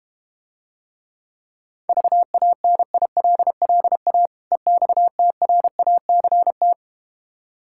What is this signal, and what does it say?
Morse code, heard as a keyed tone, about 32 wpm, 700 Hz. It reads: VANILLA EXTRACT